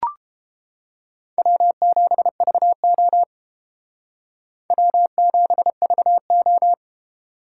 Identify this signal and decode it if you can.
Morse code, heard as a keyed tone, about 29 wpm, 700 Hz. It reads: W7VO W7VO